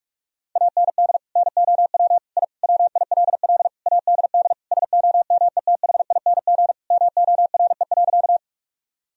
Morse code, heard as a keyed tone, about 41 wpm, 700 Hz. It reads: AND NOW I WILL ADD SOMETHING MORE.
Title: Sentences from Top 1000 Words - Review Part 1 - 45wpm